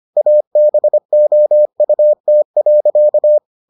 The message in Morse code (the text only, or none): ABOUT.